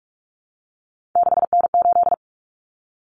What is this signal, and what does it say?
Morse code, heard as a keyed tone, about 44 wpm, 700 Hz. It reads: 6N8
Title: Common Cut Numbers - 3-Digit 45wpm